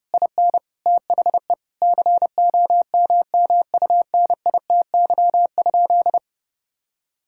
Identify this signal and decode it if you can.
Morse code, heard as a keyed tone, about 32 wpm, 700 Hz. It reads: IN THE COMMUNITY?
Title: Sentences from Top 1000 Words - Review Part 5 - 30wpm